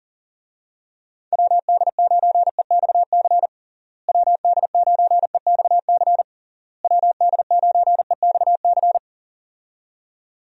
Morse code, heard as a keyed tone, about 39 wpm, 700 Hz. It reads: WD9EXC WD9EXC WD9EXC
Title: USA Call Signs - 6 Characters - Repeated 3x - 40wpm